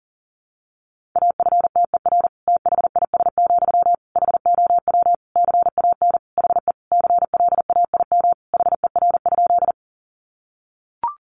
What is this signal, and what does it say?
Morse code, heard as a keyed tone, about 40 wpm, 700 Hz. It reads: AFTER THIS, HOW CAN HE CLAIM HER?
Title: Sentences from Top 1500 Words 40wpm